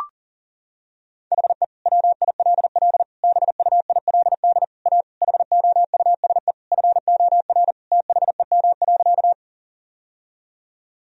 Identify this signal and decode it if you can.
Morse code, heard as a keyed tone, about 37 wpm, 700 Hz. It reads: HE WILL BUILD A HOUSE FOR THEM.
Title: Sentences from Top 1000 Words - Review Part 1 - 40wpm